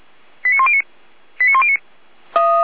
Five-tone selective call (ZVEI), heard as quick runs of five tones, tones of about 80 ms. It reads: 89109 89109